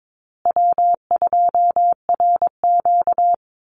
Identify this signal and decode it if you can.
Morse code, heard as a keyed tone, about 22 wpm, 700 Hz. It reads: W2RQ
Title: Call Signs Repeated 2x - 22wpm